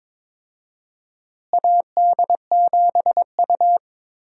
Morse code, heard as a keyed tone, about 22 wpm, 700 Hz. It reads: AD7U
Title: USA Call Signs - 4 Characters - Repeated 3x - 22wpm